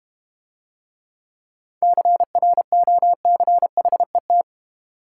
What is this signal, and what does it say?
Morse code, heard as a keyed tone, about 32 wpm, 700 Hz. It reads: CROCHET